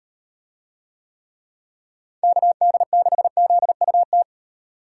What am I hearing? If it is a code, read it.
Morse code, heard as a keyed tone, about 38 wpm, 700 Hz. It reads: KD6ZUT